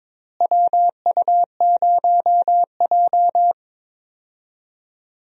Morse code, heard as a keyed tone, about 22 wpm, 700 Hz. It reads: WU0J
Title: USA Call Signs - 4 Characters - Repeated 3x - 22wpm